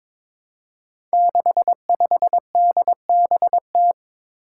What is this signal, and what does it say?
Morse code, heard as a keyed tone, about 22 wpm, 700 Hz. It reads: 65DBT
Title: Five Challenging Letter-Number Combinations 22wpm